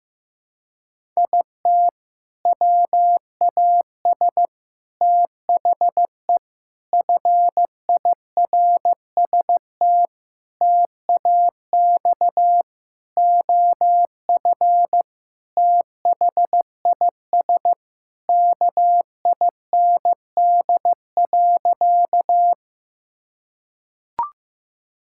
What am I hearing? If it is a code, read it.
Morse code, heard as a keyed tone, about 15 wpm, 700 Hz. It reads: IT WAS THE FIRST TAX OF THIS KIND.
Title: Sentences from Top 1000 Words - Review Part 5 - 15wpm